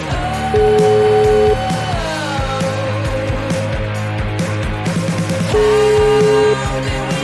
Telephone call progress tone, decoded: ringback tone

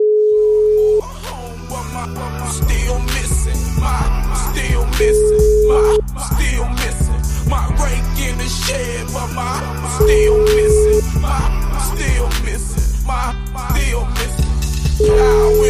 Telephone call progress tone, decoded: ringback tone

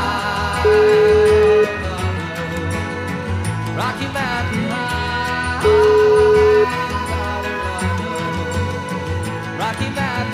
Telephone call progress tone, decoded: ringback tone

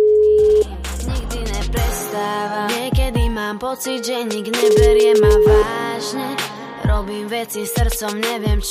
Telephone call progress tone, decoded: ringback tone

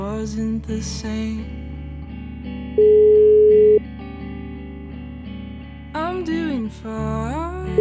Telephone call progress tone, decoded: ringback tone

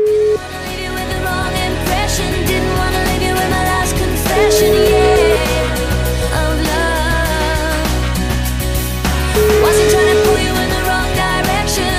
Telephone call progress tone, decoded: ringback tone

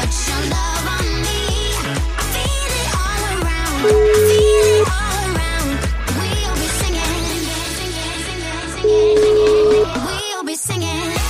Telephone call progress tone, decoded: ringback tone

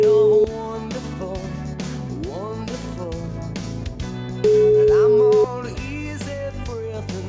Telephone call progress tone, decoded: ringback tone

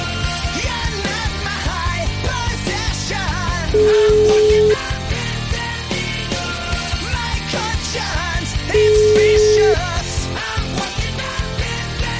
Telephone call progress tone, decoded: ringback tone

